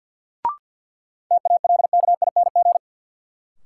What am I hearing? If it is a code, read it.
Morse code, heard as a keyed tone, about 48 wpm, 700 Hz. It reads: TALKING
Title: Sets of 2 Words - New Word Drill - 50wpm